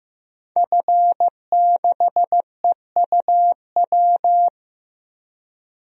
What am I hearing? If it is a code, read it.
Morse code, heard as a keyed tone, about 15 wpm, 700 Hz. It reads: F6EUW
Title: DXCC-100 Call Signs - 5 Characters - Repeated 2x - 15wpm